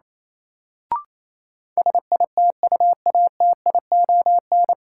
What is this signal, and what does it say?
Morse code, heard as a keyed tone, about 28 wpm, 700 Hz. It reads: SITUATION
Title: Sentences from Top 1200 Words - New Word Drill Alt - 28wpm